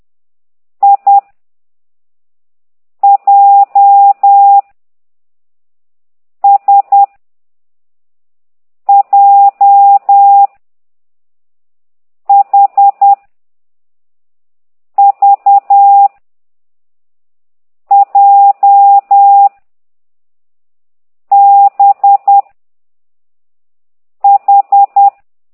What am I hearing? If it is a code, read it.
Morse code, heard as a keyed tone, about 10 wpm, 800 Hz. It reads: I J S J H V J B H
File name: ct2idw / exerc9